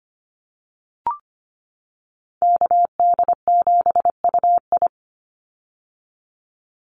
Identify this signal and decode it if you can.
Morse code, heard as a keyed tone, about 25 wpm, 700 Hz. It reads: KD7UI
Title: Call Signs - 25wpm